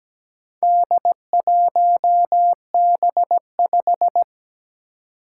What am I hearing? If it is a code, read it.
Morse code, heard as a keyed tone, about 17 wpm, 700 Hz. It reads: D1B5
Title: Four Challenging Letter-Number Combinations 17wpm